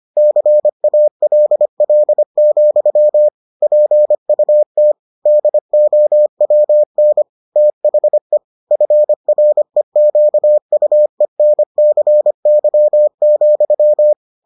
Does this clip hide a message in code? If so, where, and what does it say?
Yes: Morse code from 0.2 to 14.1 s, heard as a keyed tone, about 25 wpm, 600 Hz. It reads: CALL, PUT DOWN THE FREQUENCY,